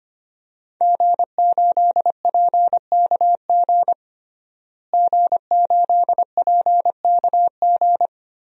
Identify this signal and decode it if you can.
Morse code, heard as a keyed tone, about 25 wpm, 700 Hz. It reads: G8PKG G8PKG
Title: DXCC-100 Call Signs - 5 Characters - Repeated 3x - 25wpm